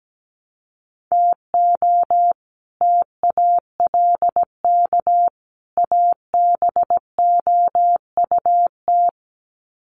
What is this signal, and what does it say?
Morse code, heard as a keyed tone, about 17 wpm, 700 Hz. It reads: TO TALK ABOUT
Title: Sets of 3 Words 17wpm